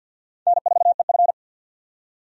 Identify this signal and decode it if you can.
Morse code, heard as a keyed tone, about 46 wpm, 700 Hz. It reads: N4EF
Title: USA Call Signs - 4 Characters - 50wpm